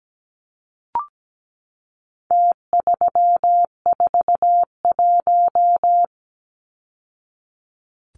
Morse code, heard as a keyed tone, about 17 wpm, 700 Hz. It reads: T341
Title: Numbers - Contest Style - 4-Digit 17wpm